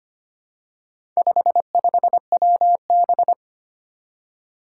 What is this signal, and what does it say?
Morse code, heard as a keyed tone, about 25 wpm, 700 Hz. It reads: 55WB
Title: Four Challenging Letter-Number Combinations 25wpm